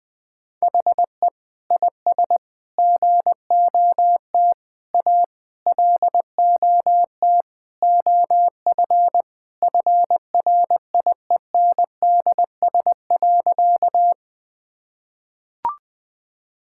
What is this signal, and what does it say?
Morse code, heard as a keyed tone, about 20 wpm, 700 Hz. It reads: HE IS GOT A LOT OF FRIENDS.
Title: Sentences with 4-Word Phrases - 20wpm